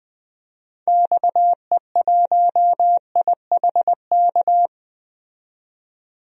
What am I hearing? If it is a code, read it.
Morse code, heard as a keyed tone, about 20 wpm, 700 Hz. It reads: XE1IHK
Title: DXCC-100 Call Signs - 6 Characters - 20wpm